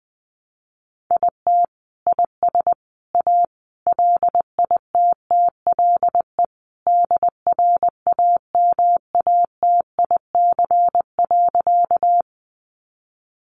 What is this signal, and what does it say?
Morse code, heard as a keyed tone, about 20 wpm, 700 Hz. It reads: IT IS A LITTLE DRAMATIC.
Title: Sentences from Top 1300 Words 20wpm